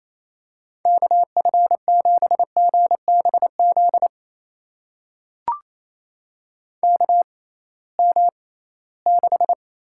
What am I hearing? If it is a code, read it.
Morse code, heard as a keyed tone, about 28 wpm, spaced as for 18 wpm, 700 Hz. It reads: KF7GBZ KM6
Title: USA Call Signs - 6 Characters - 6x Character Spacing - 28wpm